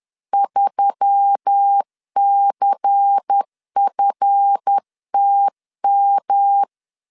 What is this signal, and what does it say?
Morse code, heard as a keyed tone, about 10 wpm, 800 Hz. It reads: 3CFTM
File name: LEZIONI / 7 GF9Z / 7 GF9 10wpm 2747